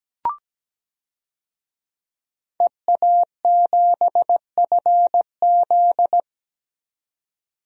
Morse code, heard as a keyed tone, about 17 wpm, 700 Hz. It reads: EA7FZ
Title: DXCC-100 Call Signs - 5 Characters - 17wpm